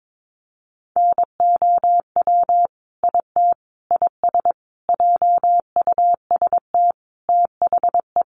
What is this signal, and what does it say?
Morse code, heard as a keyed tone, about 22 wpm, 700 Hz. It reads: NOW IT IS JUST THE
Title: Sentences from Top 1300 Words 22wpm